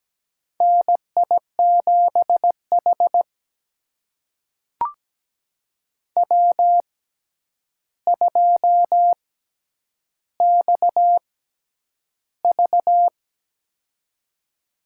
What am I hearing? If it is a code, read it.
Morse code, heard as a keyed tone, about 17 wpm, spaced as for 9 wpm, 700 Hz. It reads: NI7H W2XV
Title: USA Call Signs - 4 Characters - 6x Character Spacing - 17wpm